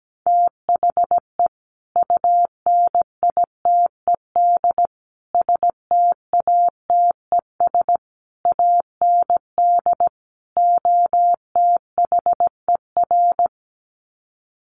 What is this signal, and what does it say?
Morse code, heard as a keyed tone, about 17 wpm, 700 Hz. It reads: THE UNITED STATES AND OTHER
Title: Sets of 5 Words Encore 17wpm